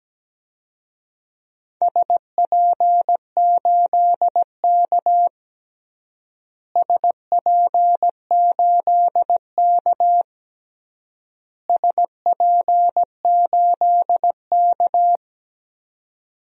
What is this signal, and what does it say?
Morse code, heard as a keyed tone, about 17 wpm, 700 Hz. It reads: SP8K SP8K SP8K
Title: DXCC-100 Call Signs - 4 Characters - Repeated 3x - 17wpm